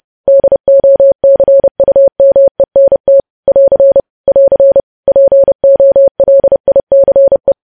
Morse code, heard as a keyed tone, about 30 wpm, 550 Hz. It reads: DOCUMENT <AR> <AR> POLICE
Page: news headlines in morse code at 30 wpm